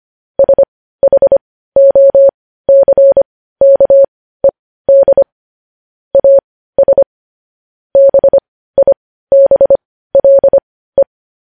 Morse code, heard as a keyed tone, about 25 wpm, spaced as for 15 wpm, 550 Hz. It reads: SHOCKED AS BIBLE
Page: news-headlines-in-morse-code-at-15-wpm